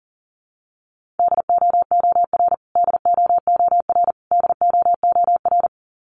Morse code, heard as a keyed tone, about 39 wpm, 700 Hz. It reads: DOOR DOOR DOOR
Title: Top 500 Words Repeated 40wpm